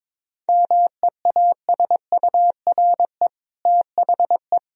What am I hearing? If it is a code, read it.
Morse code, heard as a keyed tone, about 22 wpm, 700 Hz. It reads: MEASURE THE
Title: Sentences from Top 1000 Words - Review Part 2 - 22wpm